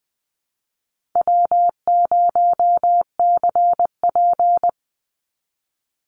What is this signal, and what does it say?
Morse code, heard as a keyed tone, about 20 wpm, 700 Hz. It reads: W0CP
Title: Easy Call Signs - 20wpm